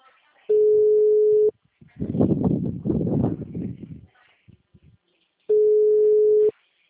Ringback tone (Europe, 425 Hz).
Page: Valinhos - SP